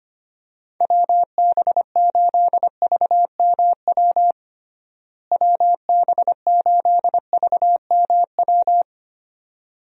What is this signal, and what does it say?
Morse code, heard as a keyed tone, about 25 wpm, 700 Hz. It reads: WB8VMW WB8VMW
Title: USA Call Signs - 6 Characters - Repeated 2x - 25wpm